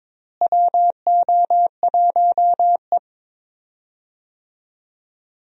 Morse code, heard as a keyed tone, about 22 wpm, 700 Hz. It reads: WO1E
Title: USA Call Signs - 4 Characters - Repeated 2x - 22wpm